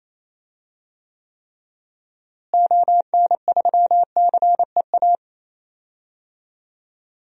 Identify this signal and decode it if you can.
Morse code, heard as a keyed tone, about 28 wpm, 700 Hz. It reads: ON3CEA
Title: DXCC-100 Call Signs - 6 Characters - Repeated 3x - 28wpm